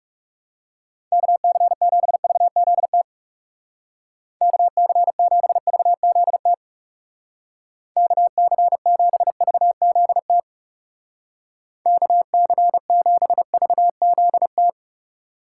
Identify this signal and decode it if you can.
Morse code, heard as a keyed tone, about 36 wpm, 700 Hz. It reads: KC7VZT KC7VZT KC7VZT KC7VZT
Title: Speed Racing - USA 2x3 Call Signs 30wpm